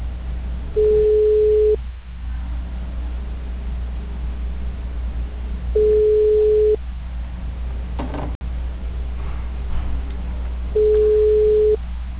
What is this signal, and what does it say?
Ringback tone (Europe, 425 Hz).